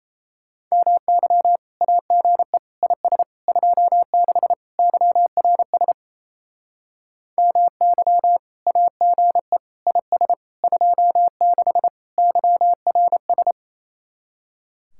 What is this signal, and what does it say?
Morse code, heard as a keyed tone, about 34 wpm, 700 Hz. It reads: MY AGE IS 26 YRS MY AGE IS 26 YRS
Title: Speed-Racing Common QSO Phrases - 28wpm